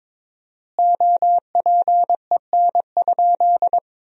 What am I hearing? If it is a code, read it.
Morse code, heard as a keyed tone, about 22 wpm, 700 Hz. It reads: OPEN?
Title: 4x Word-Spacing - Sentences from Top 300 Words - 22wpm